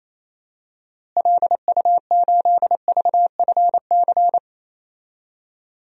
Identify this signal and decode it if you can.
Morse code, heard as a keyed tone, about 28 wpm, 700 Hz. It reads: LU8VFC
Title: DXCC-100 Call Signs - 6 Characters - Repeated 3x - 28wpm